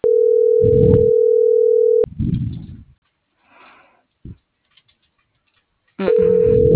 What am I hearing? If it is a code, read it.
Ringback tone (North America), 440 + 480 Hz.